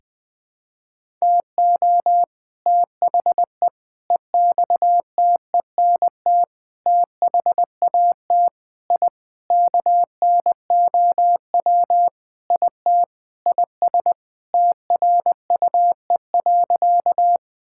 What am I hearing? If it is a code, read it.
Morse code, heard as a keyed tone, about 20 wpm, 700 Hz. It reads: TO THE EXTENT THAT I KNOW IT IS TRUE.
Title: Sentences with 4-Word Phrases - Spoken Beginning - 20wpm